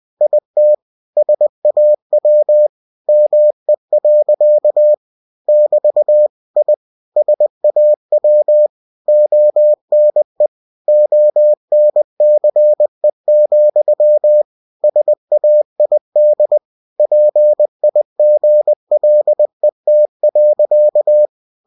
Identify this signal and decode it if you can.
Morse code, heard as a keyed tone, about 20 wpm, 600 Hz. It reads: IT SAW ME. <BT> I SAW ONE ONCE, SAID PIGLET.
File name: cw-books / winnie-the-pooh / winnie-20wpm-page-0033